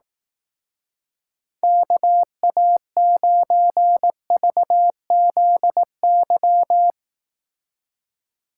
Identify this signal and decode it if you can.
Morse code, heard as a keyed tone, about 18 wpm, 700 Hz. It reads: KA9VZY
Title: Speed Racing - USA 2x3 Call Signs 15wpm